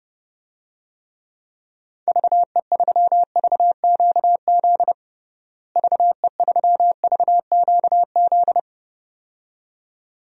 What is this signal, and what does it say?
Morse code, heard as a keyed tone, about 30 wpm, 700 Hz. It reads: VE3VQZ VE3VQZ
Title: DXCC-100 Call Signs - 6 Characters - Repeated 2x - 30wpm